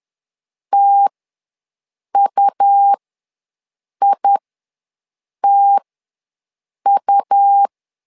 Morse code, heard as a keyed tone, about 11 wpm, spaced as for 6 wpm, 800 Hz. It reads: TUITU
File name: LEZIONI / 2 UHDN / 2 U ws 10wpm 2017